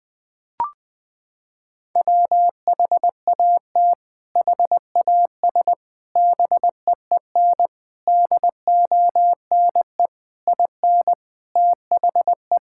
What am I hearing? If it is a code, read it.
Morse code, heard as a keyed tone, about 21 wpm, 700 Hz. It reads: WHAT HAS BEEN DONE IN THE
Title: Sentences from Top 1600 Words 20wpm